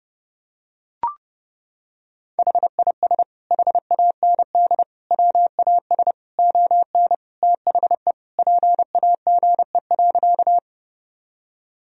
Morse code, heard as a keyed tone, about 30 wpm, 700 Hz. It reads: HIS HAND WAS ON THE PAGE.
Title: Sentences from Top 1000 Words - Review Part 1 - 30wpm